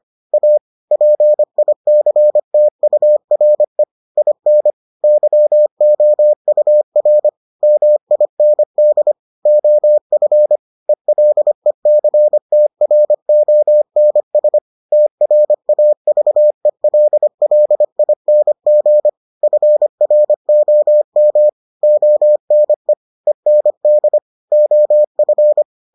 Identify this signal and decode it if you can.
Morse code, heard as a keyed tone, about 25 wpm, 600 Hz. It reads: A PICTURE IN YOUR MIND OF ELECTRONS TRAVELLING FROM ONE END OF